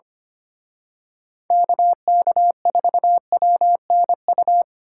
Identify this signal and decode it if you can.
Morse code, heard as a keyed tone, about 29 wpm, 700 Hz. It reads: KK4WNU
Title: Speed Racing - USA 2x3 Call Signs 25wpm